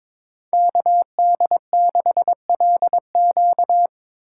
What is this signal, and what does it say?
Morse code, heard as a keyed tone, about 24 wpm, 700 Hz. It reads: KD6LQ
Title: Speed Racing - USA 2x2 Call Signs 22wpm